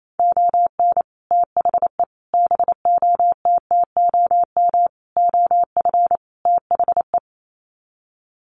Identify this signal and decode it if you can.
Morse code, heard as a keyed tone, about 28 wpm, 700 Hz. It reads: ON THE BOTTOM OF THE
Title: Sets of 5 Words Encore2 28wpm